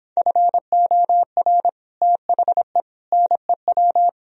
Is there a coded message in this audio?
Yes: Morse code from 0.2 to 4.1 s, heard as a keyed tone, about 26 wpm, 700 Hz. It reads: FOR THE NEW